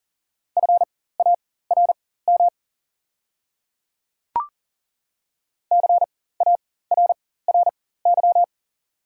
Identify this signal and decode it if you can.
Morse code, heard as a keyed tone, about 40 wpm, spaced as for 19 wpm, 690 Hz. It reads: FARM CARRY
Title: 4x Character Spacing - Top 500 Words - 40wpm